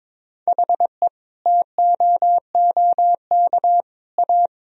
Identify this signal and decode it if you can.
Morse code, heard as a keyed tone, about 22 wpm, 700 Hz. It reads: HE TOOK A
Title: Sentences from Top 1000 Words - Review Part 1 - 22wpm